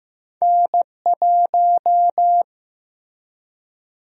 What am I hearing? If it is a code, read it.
Morse code, heard as a keyed tone, about 15 wpm, 700 Hz. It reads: N1
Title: USA Call Sign Prefixes - Repeated 2x - 15wpm